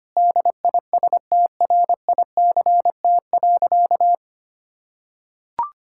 Morse code, heard as a keyed tone, about 25 wpm, 700 Hz. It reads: DISTRICT.
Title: Sentences from Top 1600 Words 25wpm